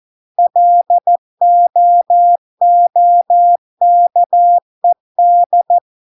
Morse code, heard as a keyed tone, about 14 wpm, 700 Hz. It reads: LOOKED